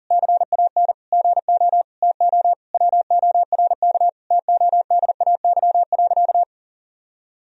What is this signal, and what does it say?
Morse code, heard as a keyed tone, about 40 wpm, 700 Hz. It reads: CAN GO TO WORK TODAY.
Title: Sentences from Top 1000 Words - Review Part 4 - 40wpm